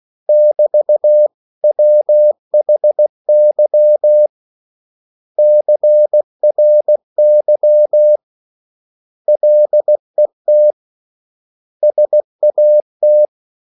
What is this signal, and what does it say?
Morse code, heard as a keyed tone, about 16 wpm, 600 Hz. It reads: <BT> WHY CRY LET SAT
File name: CW / Practice / 500MostCommonEnglishWords16